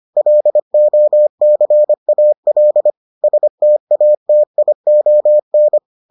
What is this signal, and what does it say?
Morse code, heard as a keyed tone, about 25 wpm, 600 Hz. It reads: LOCAL STATION